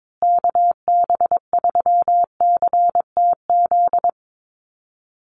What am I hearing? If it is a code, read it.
Morse code, heard as a keyed tone, about 22 wpm, 700 Hz. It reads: KB3CTZ
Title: USA Call Signs Repeated 3x 22wpm